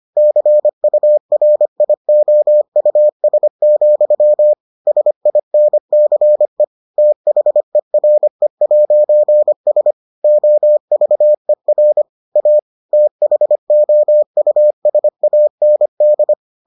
Morse code, heard as a keyed tone, about 25 wpm, 600 Hz. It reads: CURIOUS, SINCE THERE'S OVER A THOUSAND